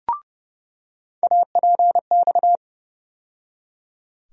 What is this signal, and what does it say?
Morse code, heard as a keyed tone, about 30 wpm, 700 Hz. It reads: APX
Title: USA Call Sign Postfixes - 3 Letters 30wpm